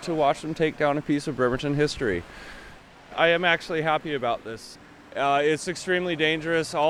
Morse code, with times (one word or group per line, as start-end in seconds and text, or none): none